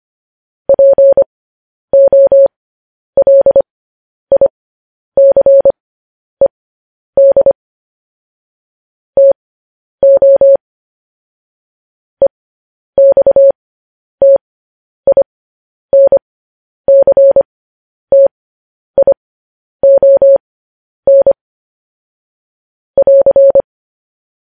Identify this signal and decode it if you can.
Morse code, heard as a keyed tone, about 25 wpm, spaced as for 10 wpm, 550 Hz. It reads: POLICED TO EXTINCTION <AR>
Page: news headlines in morse code at 10 wpm